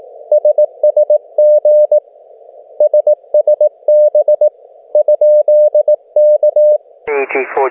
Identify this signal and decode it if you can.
Morse code, heard as a keyed tone, about 18 wpm, 600 Hz, with radio noise. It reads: SSG SSB ?K